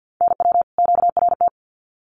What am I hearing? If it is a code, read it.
Morse code, heard as a keyed tone, about 50 wpm, 700 Hz. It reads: NW QRT
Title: Common QSO Phrases 50wpm